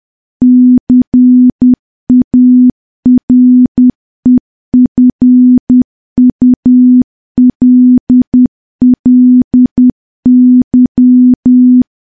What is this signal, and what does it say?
Morse code, heard as a keyed tone, about 10 wpm, 250 Hz. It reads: CAREFULLY
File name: morse / morse-training-10WPM